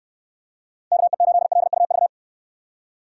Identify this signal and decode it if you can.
Morse code, heard as a keyed tone, about 64 wpm, 700 Hz. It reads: KE8LUV